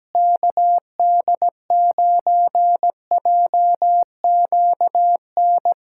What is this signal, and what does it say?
Morse code, heard as a keyed tone, about 17 wpm, 700 Hz. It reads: KD9JQN
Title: Practice QSO - Exchange 4 - Ending - 17wpm